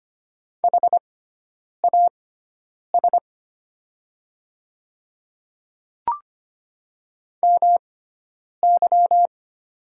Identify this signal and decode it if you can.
Morse code, heard as a keyed tone, about 25 wpm, spaced as for 9 wpm, 700 Hz. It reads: HAS MY
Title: 6x Character Spacing - Top 300 Words - 25wpm